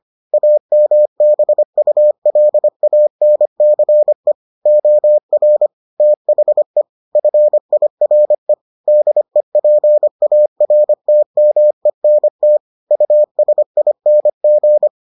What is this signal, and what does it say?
Morse code, heard as a keyed tone, about 25 wpm, 600 Hz. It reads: AMBULANCE OR THE FIRE DEPARTMENT USING